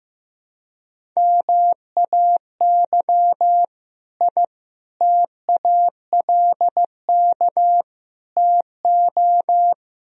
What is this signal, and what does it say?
Morse code, heard as a keyed tone, about 15 wpm, 700 Hz. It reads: MAY I TALK TO
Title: Sentences from Top 1000 Words - Review Part 4 - 15wpm